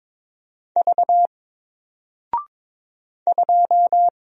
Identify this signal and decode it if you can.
Morse code, heard as a keyed tone, about 22 wpm, 700 Hz. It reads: V 2
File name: Single-Letter-Number-22wpm